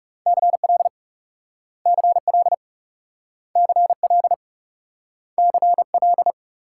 Morse code, heard as a keyed tone, about 38 wpm, 700 Hz. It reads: CL CL CL CL